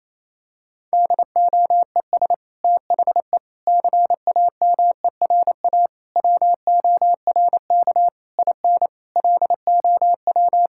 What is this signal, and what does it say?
Morse code, heard as a keyed tone, about 27 wpm, 700 Hz. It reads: DOES THE CAMERA WORK IN LOW
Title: Sentences from Top 1000 Words - Review Part 5 - 28wpm